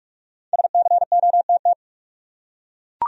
Morse code, heard as a keyed tone, about 42 wpm, 700 Hz. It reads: SCOTT
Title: USA Common Names - 45wpm